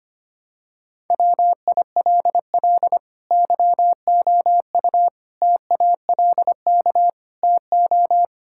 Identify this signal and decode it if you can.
Morse code, heard as a keyed tone, about 25 wpm, 700 Hz. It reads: WILL YOU TALK TO
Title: Sentences from Top 1600 Words 25wpm